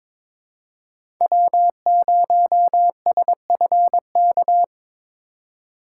Morse code, heard as a keyed tone, about 23 wpm, 700 Hz. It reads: W0SFK
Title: USA Call Signs - 5 Characters - 2x Character Spacing - 22wpm